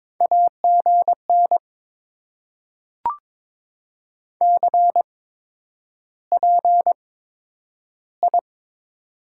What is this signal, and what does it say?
Morse code, heard as a keyed tone, about 22 wpm, spaced as for 10 wpm, 690 Hz. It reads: AGN CPI
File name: Top 90 QSO Elements - 8x Character Spacing - 22wpm